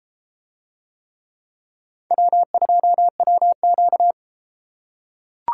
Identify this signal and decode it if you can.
Morse code, heard as a keyed tone, about 31 wpm, 700 Hz. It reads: W2WQ